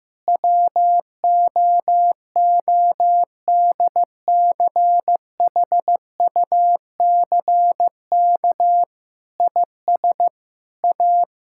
Morse code, heard as a keyed tone, about 16 wpm, 700 Hz. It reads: WOODCHUCK IS A
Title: Pangram Challenge v2 - 15wpm